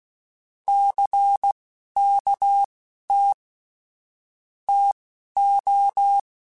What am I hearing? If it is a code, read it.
Morse code, heard as a keyed tone, about 16 wpm, spaced as for 11 wpm, 800 Hz. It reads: CKT TO
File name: Technical Subjects / Technical Subjects-mp3s / Sw Pwr Sup @10